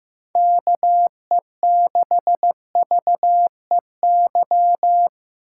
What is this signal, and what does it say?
Morse code, heard as a keyed tone, about 15 wpm, 700 Hz. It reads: KE6VEY
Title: USA Call Signs - 2x Character Spacing - 15wpm